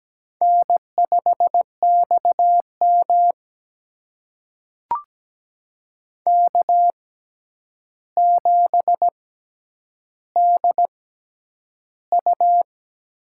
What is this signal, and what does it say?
Morse code, heard as a keyed tone, about 17 wpm, spaced as for 9 wpm, 700 Hz. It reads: N5XM K7DU
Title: USA Call Signs - 6x Character Spacing - 17wpm